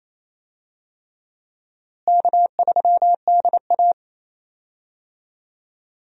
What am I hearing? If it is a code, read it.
Morse code, heard as a keyed tone, about 27 wpm, 700 Hz. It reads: K3DA
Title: USA Call Signs - 4 Characters - Repeated 3x - 28wpm